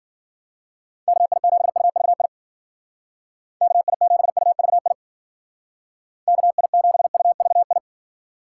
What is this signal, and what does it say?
Morse code, heard as a keyed tone, about 49 wpm, 700 Hz. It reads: KI7UVI KI7UVI KI7UVI
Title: Speed Racing - USA Call Signs 40wpm